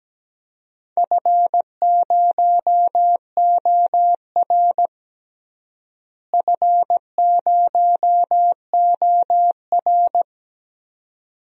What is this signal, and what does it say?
Morse code, heard as a keyed tone, about 17 wpm, 700 Hz. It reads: F0OR F0OR
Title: DXCC-100 Call Signs - 4 Characters - Repeated 2x - 17wpm